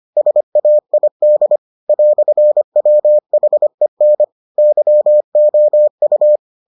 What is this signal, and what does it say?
Morse code, heard as a keyed tone, about 25 wpm, 600 Hz. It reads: SAID "WHEN YOU